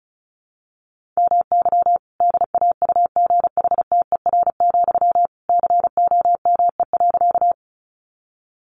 Morse code, heard as a keyed tone, about 34 wpm, 700 Hz. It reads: MY DAUGHTER, COME.
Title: Sentences from Top 1500 Words 35wpm